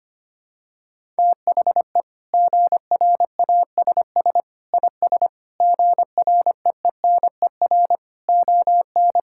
Morse code, heard as a keyed tone, about 25 wpm, 700 Hz. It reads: THE GRASS IS GREENER ON
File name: practice / 25wpm / Sentences with 4-Word Phrases - 25wpm